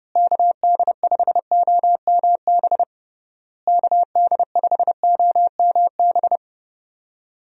Morse code, heard as a keyed tone, about 32 wpm, 700 Hz. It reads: KD5OMB KD5OMB
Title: USA Call Signs - 6 Characters - Repeated 3x - 30wpm